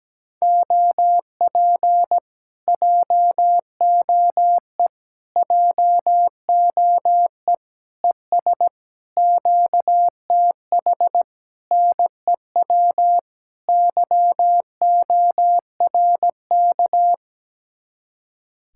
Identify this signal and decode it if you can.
Morse code, heard as a keyed tone, about 17 wpm, 700 Hz. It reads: OP JOE JOE ES QTH NEW YORK
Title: Speed-Racing Common QSO Phrases - 17wpm